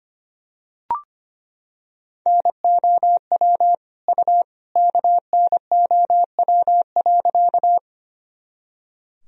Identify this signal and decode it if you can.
Morse code, heard as a keyed tone, about 25 wpm, 700 Hz. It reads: NOW U KNOW.
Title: Sentences from Top 100 Words and Abbreviations 25wpm